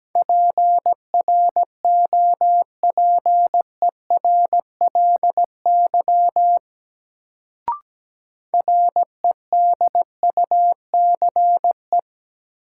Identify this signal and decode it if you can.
Morse code, heard as a keyed tone, about 17 wpm, 700 Hz. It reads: PROPERLY REDUCE
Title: Sentences from Top 1100 Words - New Word Drill Alt - 17wpm